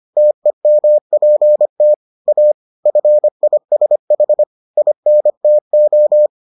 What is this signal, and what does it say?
Morse code, heard as a keyed tone, about 25 wpm, 600 Hz. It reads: TEMPT A FISH INTO